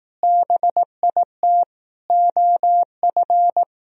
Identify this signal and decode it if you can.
Morse code, heard as a keyed tone, about 18 wpm, 700 Hz. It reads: BIT OF
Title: Speed-Racing - Sets of 4 Words 15wpm